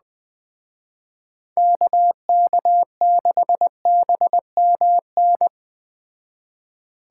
Morse code, heard as a keyed tone, about 20 wpm, 700 Hz. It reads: KK6BMN
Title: Speed Racing - USA 2x3 Call Signs 17wpm